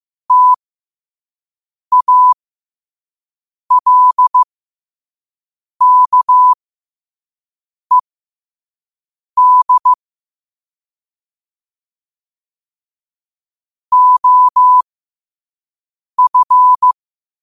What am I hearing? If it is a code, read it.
Morse code, heard as a keyed tone, about 15 wpm, spaced as for 5 wpm, 1000 Hz. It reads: TALKED OF
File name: qotdfiles / 5wpm